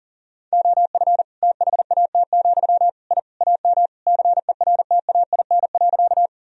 Morse code, heard as a keyed tone, about 40 wpm, 700 Hz. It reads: OF THAT, I AM CERTAIN.
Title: Sentences from Top 1000 Words - Review Part 2 - 40wpm